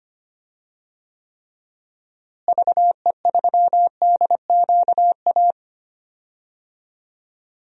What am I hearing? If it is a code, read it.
Morse code, heard as a keyed tone, about 25 wpm, 700 Hz. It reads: VE3DQA
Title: DXCC-100 Call Signs - 6 Characters - Repeated 3x - 25wpm